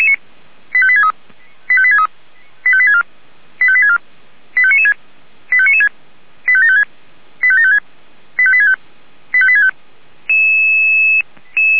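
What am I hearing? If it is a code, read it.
Five-tone selective call (ZVEI), heard as quick runs of five tones, tones of about 70 ms. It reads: 85863 85863 85864 85864 85996 85996 85756 85756 85865 85865
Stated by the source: ZVEI five-tone selective call, code 85862, received with an SDR.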